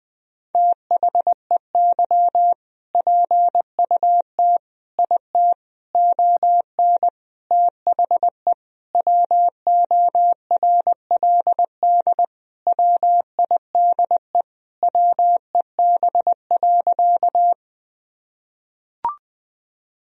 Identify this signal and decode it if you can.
Morse code, heard as a keyed tone, about 20 wpm, 700 Hz. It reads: THEY PUT IT ON THE WORLD WIDE WEB.
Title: Sentences from Top 1000 Words - Review Part 4 - 20wpm